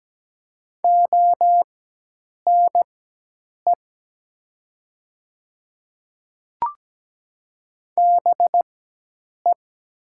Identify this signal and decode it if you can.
Morse code, heard as a keyed tone, about 17 wpm, spaced as for 8 wpm, 700 Hz. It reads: ONE BE
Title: 4x Character Spacing - Top 300 Words - 17wpm